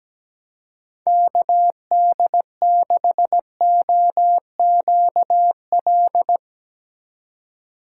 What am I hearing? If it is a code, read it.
Morse code, heard as a keyed tone, about 17 wpm, 700 Hz. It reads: KD6OQL
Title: USA Call Signs - 4x Character Spacing - 17wpm